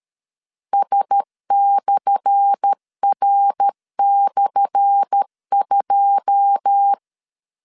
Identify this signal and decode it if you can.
Morse code, heard as a keyed tone, about 13 wpm, 800 Hz. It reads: S/R/2
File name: LEZIONI / 10 SIGNS / 10A SLASH 12wpm 2314